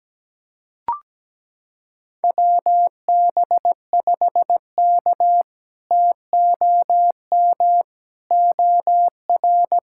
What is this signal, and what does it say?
Morse code, heard as a keyed tone, about 17 wpm, 700 Hz. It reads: WB5K TOM OR
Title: K1USN SST QSO Practice - 17wpm